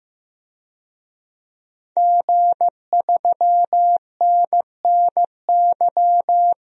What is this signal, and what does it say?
Morse code, heard as a keyed tone, about 15 wpm, 700 Hz. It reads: G3NNY
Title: DXCC-100 Call Signs - 5 Characters - 15wpm